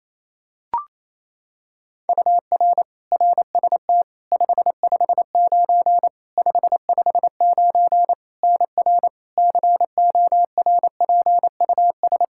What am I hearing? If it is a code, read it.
Morse code, heard as a keyed tone, about 28 wpm, 700 Hz. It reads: UR RST 559 559 NR CORPUS
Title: K1USN SST QSO Practice 28wpm